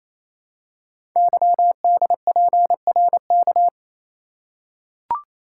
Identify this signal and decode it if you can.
Morse code, heard as a keyed tone, about 27 wpm, 700 Hz. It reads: YDPRK
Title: Code Groups - 5 Letters - 28wpm